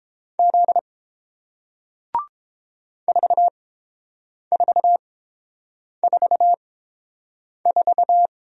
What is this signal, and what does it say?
Morse code, heard as a keyed tone, about 31 wpm, spaced as for 7 wpm, 700 Hz. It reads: Z 4444